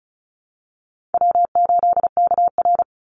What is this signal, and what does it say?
Morse code, heard as a keyed tone, about 34 wpm, 700 Hz. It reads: W8KR
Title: Call Signs - 35wpm